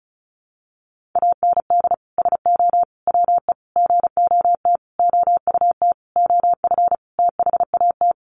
Morse code, heard as a keyed tone, about 34 wpm, 700 Hz. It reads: AND SO WE GOT OUT OF THAT
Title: Sentences from Top 1500 Words 35wpm